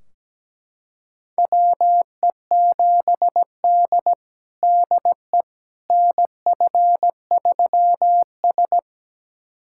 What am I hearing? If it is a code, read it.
Morse code, heard as a keyed tone, about 17 wpm, 700 Hz. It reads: WE7D DE NF3S
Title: USA Callsign DE Callsign - 4 Characters - 17wpm